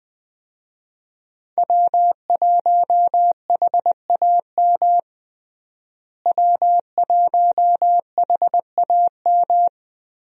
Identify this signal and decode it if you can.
Morse code, heard as a keyed tone, about 21 wpm, 700 Hz. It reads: W1HAM W1HAM
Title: USA Call Signs - 5 Characters - Repeated 2x - 20wpm